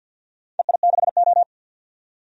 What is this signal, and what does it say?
Morse code, heard as a keyed tone, about 46 wpm, 700 Hz. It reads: EI6O
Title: DXCC-100 Call Signs - 4 Characters - 50wpm